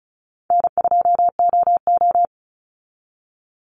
Morse code, heard as a keyed tone, about 34 wpm, 700 Hz. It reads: N2OO
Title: Call Signs Repeated 3x - 35wpm